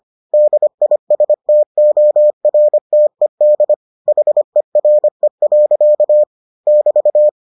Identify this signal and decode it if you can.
Morse code, heard as a keyed tone, about 25 wpm, 600 Hz. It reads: DISTORTED HERE. <BT>